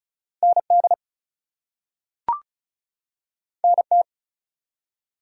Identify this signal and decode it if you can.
Morse code, heard as a keyed tone, about 36 wpm, 700 Hz. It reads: ND NT
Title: US States and Canadian Provinces 35wpm